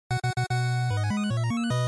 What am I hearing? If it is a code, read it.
There is no code.